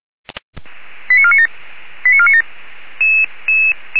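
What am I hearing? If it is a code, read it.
Five-tone selective call (ZVEI), heard as quick runs of five tones, tones of about 70 ms. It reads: 89497 89497
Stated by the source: ZVEI five-tone call 85371 received by SDR.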